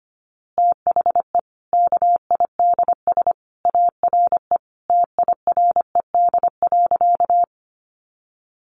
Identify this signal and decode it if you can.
Morse code, heard as a keyed tone, about 25 wpm, 700 Hz. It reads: THE KIDS ARE TIRED.
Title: Easy Phrases 25wpm